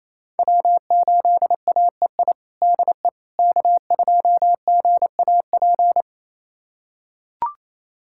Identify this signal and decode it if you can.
Morse code, heard as a keyed tone, about 28 wpm, 700 Hz. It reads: W8AEI DE K2GAP
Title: USA Callsign DE Callsign - 5 Characters - 28wpm